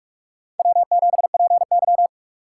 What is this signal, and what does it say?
Morse code, heard as a keyed tone, about 44 wpm, 700 Hz. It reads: W7PY
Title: USA Call Signs - 4 Characters - 3x Character Spacing - 45wpm